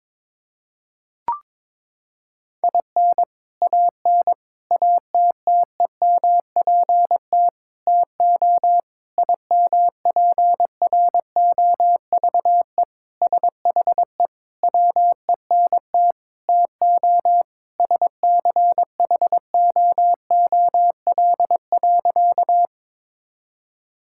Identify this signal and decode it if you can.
Morse code, heard as a keyed tone, about 22 wpm, 700 Hz. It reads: IN AN ATTEMPT TO IMPROVE SHE WENT TO SCHOOL.
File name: practice / 22wpm / Sentences with 4-Word Phrases - 22wpm